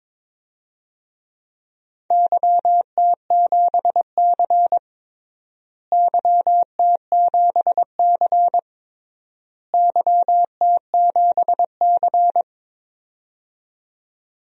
Morse code, heard as a keyed tone, about 22 wpm, 700 Hz. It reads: YT7C YT7C YT7C
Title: DXCC-100 Call Signs - 4 Characters - Repeated 3x - 22wpm